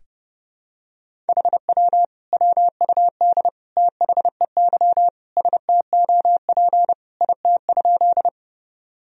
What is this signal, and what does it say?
Morse code, heard as a keyed tone, about 30 wpm, 700 Hz. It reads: HW WUD THEY STOP IT?
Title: Sentences from Top 300 Words and Abbreviations 30wpm